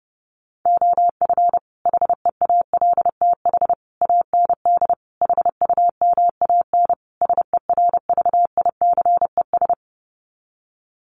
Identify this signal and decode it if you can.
Morse code, heard as a keyed tone, about 28 wpm, 700 Hz. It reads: OF HEALTH AND HUMAN SERVICES
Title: Sets of 5 Words 30wpm